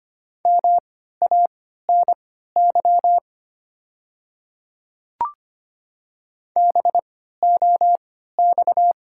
Morse code, heard as a keyed tone, about 25 wpm, spaced as for 14 wpm, 690 Hz. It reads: MANY BOX
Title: 3x Character Spacing - Top 500 Words - 25wpm